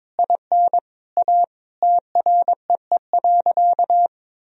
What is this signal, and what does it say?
Morse code, heard as a keyed tone, about 22 wpm, 700 Hz. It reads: IN A TREE.
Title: Sentences from Top 1000 Words - Review Part 1 - 22wpm